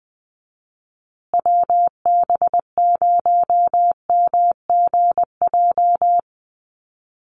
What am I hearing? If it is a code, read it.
Morse code, heard as a keyed tone, about 20 wpm, 700 Hz. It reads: WB0MGJ
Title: USA Call Signs Repeated 2x 20wpm